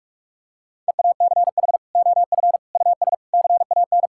Morse code, heard as a keyed tone, about 46 wpm, 700 Hz. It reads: EACH OF US CAN
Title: Sentences from Top 1000 Words - Review Part 5 - 45wpm